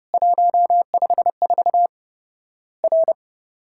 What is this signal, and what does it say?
Morse code, heard as a keyed tone, about 30 wpm, 700 Hz. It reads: 154 R
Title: Worked All Europe DX Contest - QTC Exchange - 30wpm